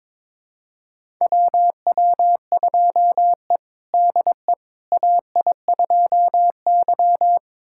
Morse code, heard as a keyed tone, about 21 wpm, 700 Hz. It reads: WW2E DE AI2Y
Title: USA Callsign DE Callsign - 4 Characters - 22wpm